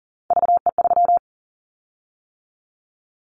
Morse code, heard as a keyed tone, about 42 wpm, 700 Hz. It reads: VE3
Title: Callsign Prefixes for Top 25 DXCC - 40wpm